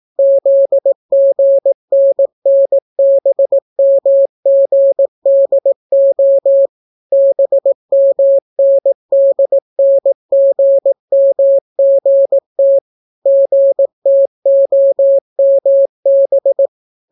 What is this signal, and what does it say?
Morse code, heard as a keyed tone, about 18 wpm, 550 Hz. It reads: ZGNNBMGDO BMNDNGMGT GTOMB